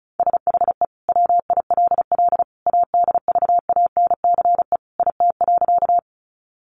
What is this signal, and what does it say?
Morse code, heard as a keyed tone, about 34 wpm, 700 Hz. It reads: SHE WILL ADVANCE IT.
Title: Sentences from Top 1400 Words 35wpm